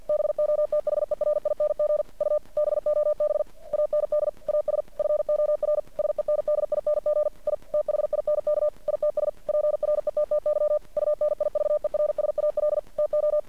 Morse code, heard as a keyed tone, about 49 wpm, 600 Hz. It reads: BOTHERING U BOB AND AS FOR SENDING I THINK ITS PRETTY UNIVERSAL TO
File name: archived / CRSnet